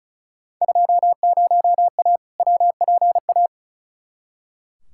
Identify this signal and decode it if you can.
Morse code, heard as a keyed tone, about 36 wpm, 700 Hz. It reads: 20A WPA
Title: Field Day Exchanges Encore - 35wpm